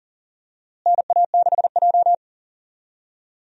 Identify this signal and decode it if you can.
Morse code, heard as a keyed tone, about 40 wpm, 700 Hz. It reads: NA6J
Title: USA Call Signs - 4 Characters - Repeated 3x - 40wpm